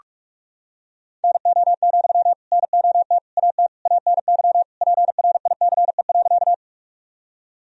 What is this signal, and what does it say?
Morse code, heard as a keyed tone, about 46 wpm, 700 Hz. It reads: NO, NOT AT ANY PRICE.
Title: Sentences from Top 1000 Words - Review Part 4 - 45wpm